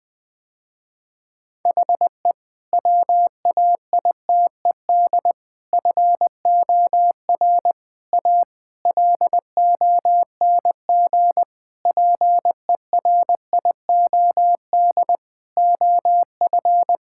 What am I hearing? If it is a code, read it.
Morse code, heard as a keyed tone, about 20 wpm, 700 Hz. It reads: HE WAITED FOR A LONG PERIOD OF
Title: Sentences with 5-Word Phrases - Spoken Beginning - 20wpm